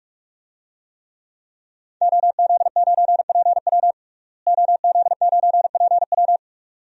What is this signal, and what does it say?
Morse code, heard as a keyed tone, about 48 wpm, 700 Hz. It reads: OZ9PW OZ9PW
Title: DXCC-100 Call Signs - 5 Characters - Repeated 2x - 45wpm